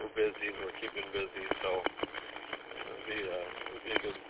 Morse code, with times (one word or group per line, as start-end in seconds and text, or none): none